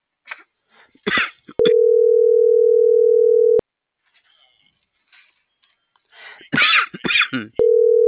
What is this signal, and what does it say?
Ringback tone (North America), 440 + 480 Hz.